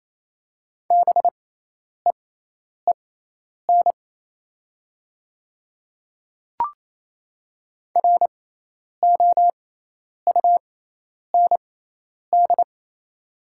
Morse code, heard as a keyed tone, about 28 wpm, spaced as for 10 wpm, 700 Hz. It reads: BEEN ROUND
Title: 6x Character Spacing - Top 200 Words - 28wpm